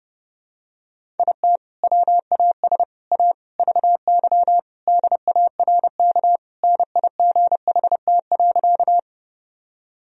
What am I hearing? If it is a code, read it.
Morse code, heard as a keyed tone, about 30 wpm, 700 Hz. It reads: IT WAS A VY DARK NIGHT.
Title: Sentences from Top 500 Words and Abbreviations 30wpm